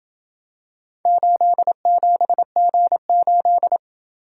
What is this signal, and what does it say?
Morse code, heard as a keyed tone, about 27 wpm, 700 Hz. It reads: 87G8